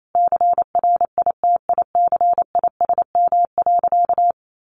Morse code, heard as a keyed tone, about 28 wpm, 700 Hz. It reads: CRITICISM.
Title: Sentences from Top 1300 Words 28wpm